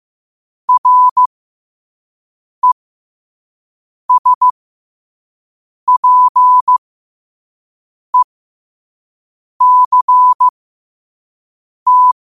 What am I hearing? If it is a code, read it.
Morse code, heard as a keyed tone, about 15 wpm, spaced as for 5 wpm, 1000 Hz. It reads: RESPECT
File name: qotdfiles / 5wpm